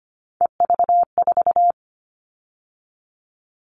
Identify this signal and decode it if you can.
Morse code, heard as a keyed tone, about 25 wpm, 700 Hz. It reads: EV4
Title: All Cut Numbers - 3-Digit 25wpm